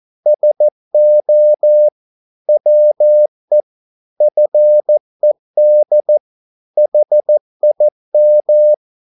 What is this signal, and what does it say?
Morse code, heard as a keyed tone, about 14 wpm, 600 Hz. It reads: SO WE FED HIM
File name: mp3 files / a dogs mistake - 14wpm